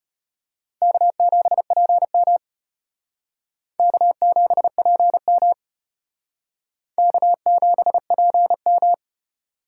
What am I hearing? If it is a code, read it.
Morse code, heard as a keyed tone, about 32 wpm, 700 Hz. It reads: K7PM K7PM K7PM